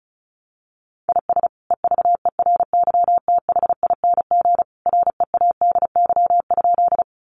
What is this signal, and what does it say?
Morse code, heard as a keyed tone, about 34 wpm, 700 Hz. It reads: IS EVERYTHING READY?
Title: Sentences from Top 700 Words 35wpm